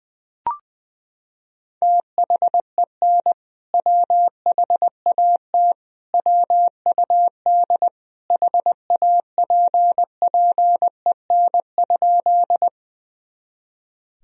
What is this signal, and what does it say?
Morse code, heard as a keyed tone, about 20 wpm, 700 Hz. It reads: THEN WHAT WUD HAPPEN?
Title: Sentences from Top 400 Words and Abbreviations 20wpm